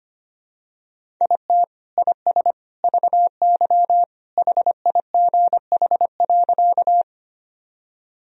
Morse code, heard as a keyed tone, about 25 wpm, 700 Hz. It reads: IT IS VY HIGH.
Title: Sentences from Top 200 Words and Abbreviations 25wpm